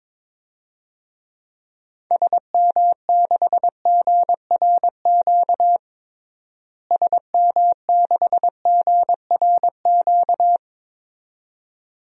Morse code, heard as a keyed tone, about 22 wpm, 700 Hz. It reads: SM6GRQ SM6GRQ
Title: DXCC-100 Call Signs - 6 Characters - Repeated 2x - 22wpm